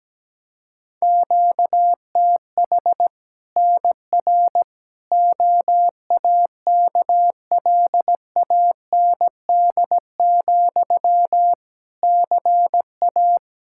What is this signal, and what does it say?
Morse code, heard as a keyed tone, about 17 wpm, 700 Hz. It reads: QTH NR OAKLAND, CA
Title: K1USN SST QSO Practice 17wpm